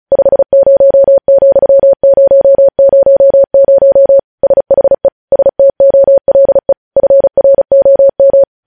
Morse code, heard as a keyed tone, about 36 wpm, 550 Hz. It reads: 50,000 SHE STOLE FROM
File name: podcast / News headlines-1x-35x35